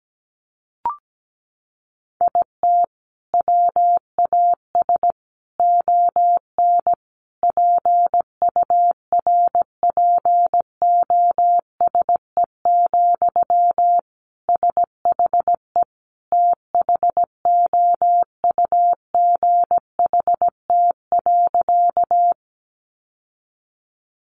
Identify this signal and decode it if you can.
Morse code, heard as a keyed tone, about 17 wpm, 700 Hz. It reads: IT WAS ON PURPOSE, SHE THOUGHT.
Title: Sentences from Top 800 Words 17wpm